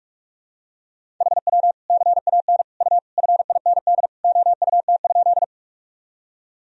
Morse code, heard as a keyed tone, about 41 wpm, 700 Hz. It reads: HW CAN U FIND OUT?
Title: Sentences from Top 200 Words and Abbreviations 45wpm